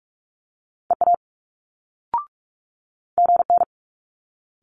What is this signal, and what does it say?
Morse code, heard as a keyed tone, about 46 wpm, 700 Hz. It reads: EA GN
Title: Multi-letter Phonemes Quick Fire 45wpm